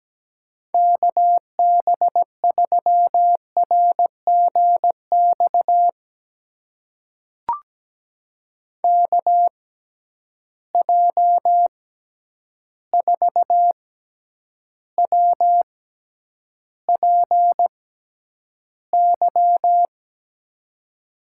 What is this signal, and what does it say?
Morse code, heard as a keyed tone, about 17 wpm, spaced as for 9 wpm, 700 Hz. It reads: KB3RGX KJ4WPY
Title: USA Call Signs - 6 Characters - 6x Character Spacing - 17wpm